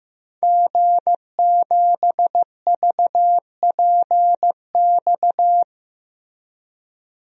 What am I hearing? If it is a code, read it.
Morse code, heard as a keyed tone, about 15 wpm, 700 Hz. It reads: G7VPX
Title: DXCC-100 Call Signs - 5 Characters - Repeated 3x - 15wpm